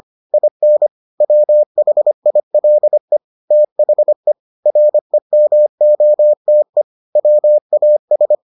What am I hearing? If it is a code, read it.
Morse code, heard as a keyed tone, about 25 wpm, 600 Hz. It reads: IN WHILE THE REMOTE WAS